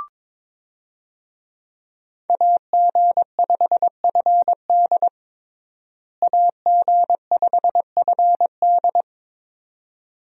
Morse code, heard as a keyed tone, about 22 wpm, 700 Hz. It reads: AG5FD AG5FD
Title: USA Call Signs - 5 Characters - Repeated 2x - 22wpm